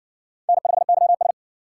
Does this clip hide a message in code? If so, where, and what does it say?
Yes: Morse code from 0.5 to 1.3 s, heard as a keyed tone, about 58 wpm, 700 Hz. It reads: N5YS